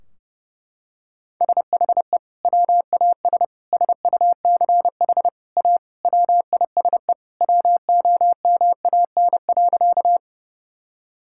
Morse code, heard as a keyed tone, about 32 wpm, 700 Hz. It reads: SHE WAS SUCH A WISE WOMAN.
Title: Sentences from Top 1600 Words 30wpm